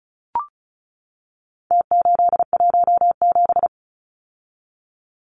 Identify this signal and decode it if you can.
Morse code, heard as a keyed tone, about 34 wpm, 700 Hz. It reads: T817
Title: Numbers - Contest Style - 4-Digit 35wpm